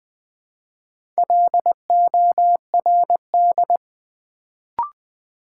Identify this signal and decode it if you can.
Morse code, heard as a keyed tone, about 20 wpm, 700 Hz. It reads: LORD
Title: Build A Word - 2 Letters At A Time - 4-Letter Words - 20wpm